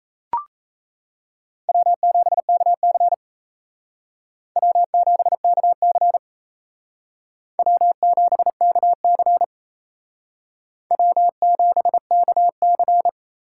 Morse code, heard as a keyed tone, about 36 wpm, 700 Hz. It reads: W7KC W7KC W7KC W7KC